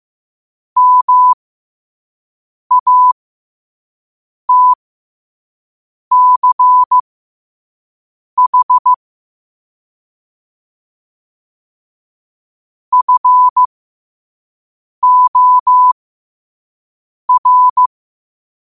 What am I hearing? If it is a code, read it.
Morse code, heard as a keyed tone, about 15 wpm, spaced as for 5 wpm, 1000 Hz. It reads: MATCH FOR
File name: qotdfiles / 5wpm